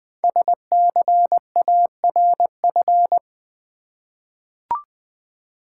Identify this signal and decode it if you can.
Morse code, heard as a keyed tone, about 20 wpm, 700 Hz. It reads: SCARF
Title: Easy Words with Hints - 20wpm